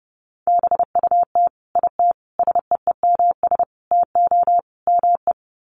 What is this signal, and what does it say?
Morse code, heard as a keyed tone, about 30 wpm, 700 Hz. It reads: BUT IT SEEMS TO ME
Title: Sets of 5 Words Encore2 30wpm